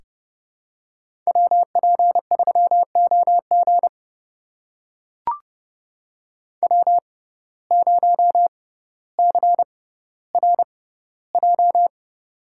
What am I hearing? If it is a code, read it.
Morse code, heard as a keyed tone, about 28 wpm, spaced as for 15 wpm, 700 Hz. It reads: WP3OG W0CRJ
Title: USA Call Signs - 5 Characters - 6x Character Spacing - 30wpm